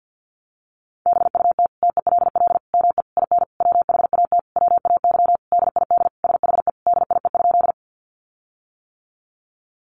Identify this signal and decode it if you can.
Morse code, heard as a keyed tone, about 49 wpm, 700 Hz. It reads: BUT TELL ME IN WHAT WAY DID SHE DIE?
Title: Sentences from Top 1300 Words 50wpm